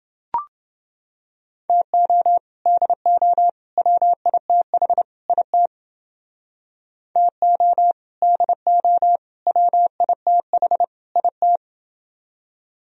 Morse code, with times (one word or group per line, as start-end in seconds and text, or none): 1.7-2.4 TO
2.7-3.5 DO
3.8-5.0 WITH
5.3-5.7 IT
7.2-7.9 TO
8.2-9.2 DO
9.5-10.8 WITH
11.2-11.6 IT